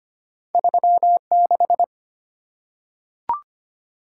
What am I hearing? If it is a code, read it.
Morse code, heard as a keyed tone, about 25 wpm, 700 Hz. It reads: 36
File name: Two Letter-Number Combinations - 25wpm